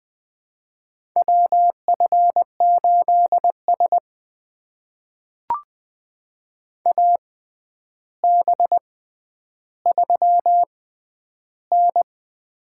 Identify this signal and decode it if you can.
Morse code, heard as a keyed tone, about 20 wpm, spaced as for 10 wpm, 700 Hz. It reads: WF8S AB3N
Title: USA Call Signs - 4 Characters - 6x Character Spacing - 20wpm